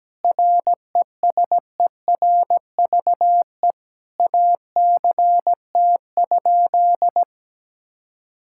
Morse code, heard as a keyed tone, about 17 wpm, 700 Hz. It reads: RESERVE ACT?
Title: Sentences from Top 1000 Words - Review Part 6 - 17wpm